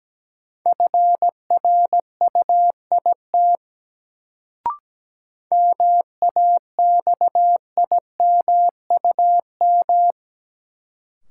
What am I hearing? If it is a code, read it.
Morse code, heard as a keyed tone, about 17 wpm, 700 Hz. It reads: FRUIT MAXIMUM
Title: Sentences from Top 1300 Words - New Word Drill Alt - 17wpm